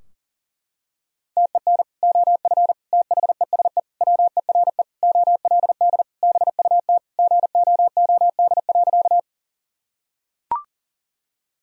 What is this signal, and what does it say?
Morse code, heard as a keyed tone, about 40 wpm, 700 Hz. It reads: TEN OF THESE WERE OLD BUT GOOD.
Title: Sentences from Top 1000 Words - Review Part 2 - 40wpm